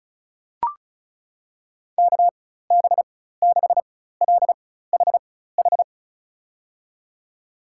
Morse code, heard as a keyed tone, about 36 wpm, spaced as for 16 wpm, 700 Hz. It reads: KB6LHH
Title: USA Call Signs - 6 Characters - 4x Character Spacing - 35wpm